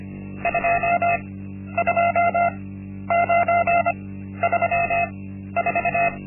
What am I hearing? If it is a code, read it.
Morse code, heard as a keyed tone, about 25 wpm, spaced as for 12 wpm, 690 Hz. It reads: 22934
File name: Linksmakalnis-1